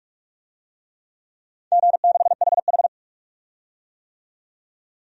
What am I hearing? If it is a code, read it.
Morse code, heard as a keyed tone, about 46 wpm, 700 Hz. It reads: G6HH
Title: DXCC-100 Call Signs - 4 Characters - 45wpm